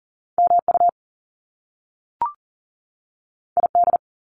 Morse code, heard as a keyed tone, about 41 wpm, 700 Hz. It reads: MU ID
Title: Top Two Letter Combinations in Words 40wpm